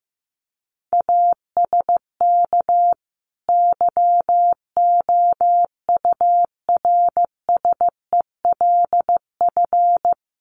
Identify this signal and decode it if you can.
Morse code, heard as a keyed tone, about 15 wpm, 700 Hz. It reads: ASK YOURSELF
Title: Sentences from Top 600 Words 15wpm